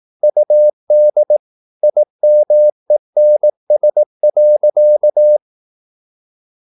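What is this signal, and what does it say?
Morse code, heard as a keyed tone, about 18 wpm, 600 Hz. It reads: UD IMENS.
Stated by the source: Slow (QRS) Morse practice at 18 wpm, CW med Gnister.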